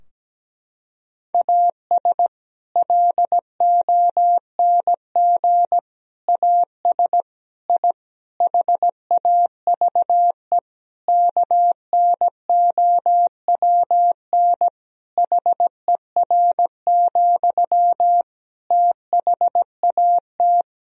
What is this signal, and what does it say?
Morse code, heard as a keyed tone, about 17 wpm, 700 Hz. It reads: AS LONG AS I HAVE KNOWN HER, THAT
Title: Sentences from Top 1000 Words - Review Part 3 - 17wpm